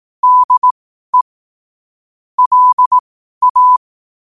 Morse code, heard as a keyed tone, about 18 wpm, spaced as for 12 wpm, 1000 Hz. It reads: DE LA
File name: qotdfiles / 12wpm